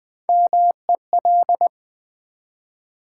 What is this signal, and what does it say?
Morse code, heard as a keyed tone, about 20 wpm, 700 Hz. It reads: MEL
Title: Build A Word - 1 Letter At A Time - 4-Letter Words - 20wpm